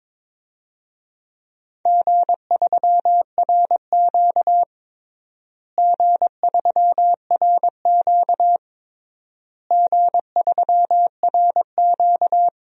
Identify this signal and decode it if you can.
Morse code, heard as a keyed tone, about 22 wpm, 700 Hz. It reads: G3RQ G3RQ G3RQ
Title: DXCC-100 Call Signs - 4 Characters - Repeated 3x - 22wpm